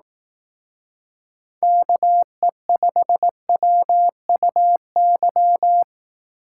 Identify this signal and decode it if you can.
Morse code, heard as a keyed tone, about 18 wpm, 700 Hz. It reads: KE5WUY
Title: Speed Racing - USA 2x3 Call Signs 15wpm